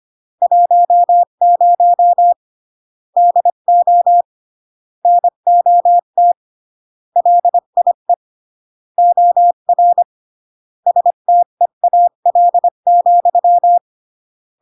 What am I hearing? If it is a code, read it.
Morse code, heard as a keyed tone, about 25 wpm, 700 Hz. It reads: 10 DO NOT LIE OR STEAL,